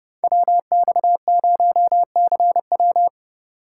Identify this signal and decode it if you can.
Morse code, heard as a keyed tone, about 37 wpm, 700 Hz. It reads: WX0CW
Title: Speed Racing - USA Call Signs 30wpm